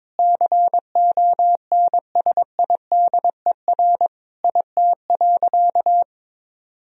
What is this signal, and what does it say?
Morse code, heard as a keyed tone, about 21 wpm, 700 Hz. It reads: CONSIDER IT.
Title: Sentences from Top 1000 Words - Review Part 4 - 22wpm